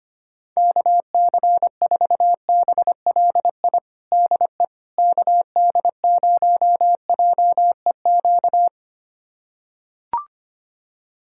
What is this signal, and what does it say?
Morse code, heard as a keyed tone, about 25 wpm, 700 Hz. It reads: KC4BLI DE KD0JEQ
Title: USA Callsign DE Callsign 25wpm